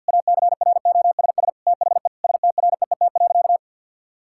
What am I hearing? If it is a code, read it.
Morse code, heard as a keyed tone, about 50 wpm, 700 Hz. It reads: ACROSS THE STREET.
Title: Sentences from Top 1000 Words - Review Part 5 - 50wpm